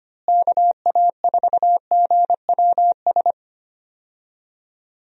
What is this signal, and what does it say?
Morse code, heard as a keyed tone, about 25 wpm, 700 Hz. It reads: KA4GWS
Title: USA Call Signs - 6 Characters - Repeated 2x - 25wpm